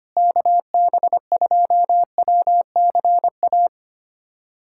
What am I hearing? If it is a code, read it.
Morse code, heard as a keyed tone, about 25 wpm, 700 Hz. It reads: KB2WCA
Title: USA Callsign DE Callsign 25wpm